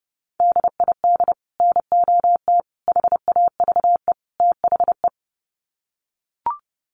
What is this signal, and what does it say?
Morse code, heard as a keyed tone, about 30 wpm, 700 Hz. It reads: DID NOT HAVE THE
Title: Sets of 4 Words 30wpm